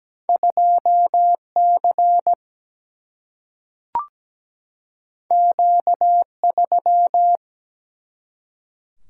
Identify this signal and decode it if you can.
Morse code, heard as a keyed tone, about 17 wpm, 700 Hz. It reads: 2C Q3
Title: Single-Letter Single-Number 17wpm